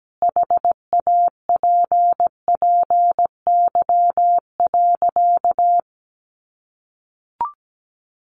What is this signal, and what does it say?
Morse code, heard as a keyed tone, about 17 wpm, 700 Hz. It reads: HAPPY.
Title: Sentences from Top 1400 Words 17wpm